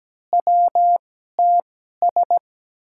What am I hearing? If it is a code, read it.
Morse code, heard as a keyed tone, about 17 wpm, spaced as for 12 wpm, 700 Hz. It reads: WTS
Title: USA Call Signs - 5 Characters - 2x Character Spacing - 17wpm